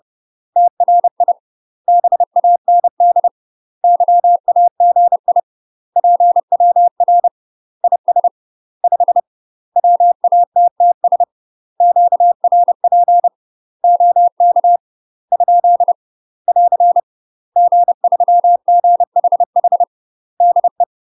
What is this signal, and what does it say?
Morse code, heard as a keyed tone, about 30 wpm, 700 Hz. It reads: TRI BAND YAGI PWR IS 5 WATTS QRP OK ? <AR> G3GHH DE
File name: UK MASTER QSOs TRAINING / 30WPM / UK MASTER QSOs 0051